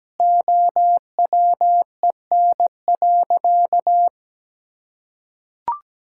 Morse code, heard as a keyed tone, about 17 wpm, 700 Hz. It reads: OWEN.
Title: Pangram Challenge v2 - 17wpm